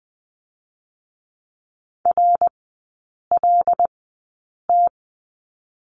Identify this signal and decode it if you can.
Morse code, heard as a keyed tone, about 20 wpm, spaced as for 8 wpm, 700 Hz. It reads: RLT
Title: 152 Lesson 25e - Letter p - Three Characters 20wpm